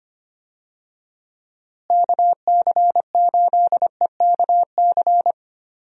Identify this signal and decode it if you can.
Morse code, heard as a keyed tone, about 25 wpm, 700 Hz. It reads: KC8EKC
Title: USA Call Signs - 6 Characters - Repeated 2x - 25wpm